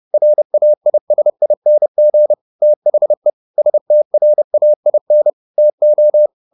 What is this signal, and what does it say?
Morse code, heard as a keyed tone, about 30 wpm, 600 Hz. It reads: RAISING THE STRAIN TO